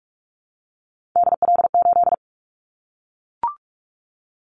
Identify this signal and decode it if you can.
Morse code, heard as a keyed tone, about 44 wpm, 700 Hz. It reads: DL8
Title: Callsign Prefixes for Top 25 DXCC - 45wpm